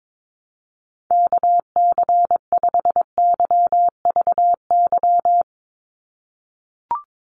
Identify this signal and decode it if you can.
Morse code, heard as a keyed tone, about 22 wpm, 700 Hz. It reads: KC5YVY
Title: USA Call Signs 22wpm